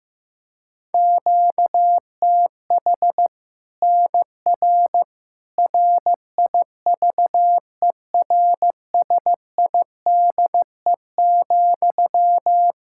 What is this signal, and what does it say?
Morse code, heard as a keyed tone, about 15 wpm, 700 Hz. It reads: QTH NR RIVERSIDE,
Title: Practice QSO - Exchange 1 - Beginning - 15wpm